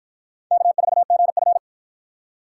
Morse code, heard as a keyed tone, about 47 wpm, 700 Hz. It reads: K4GF